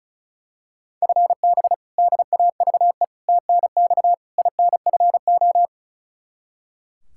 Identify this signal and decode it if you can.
Morse code, heard as a keyed tone, about 36 wpm, 700 Hz. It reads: FB DAVE TNX INFO
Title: K1USN SST QSO Practice 35wpm